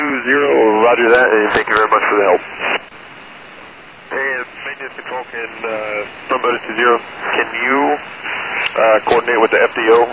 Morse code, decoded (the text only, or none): none